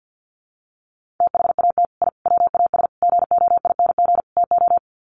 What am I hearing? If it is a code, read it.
Morse code, heard as a keyed tone, about 50 wpm, 700 Hz. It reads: THAT I WAS GOING TO
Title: Sets of 5 Words Encore 50wpm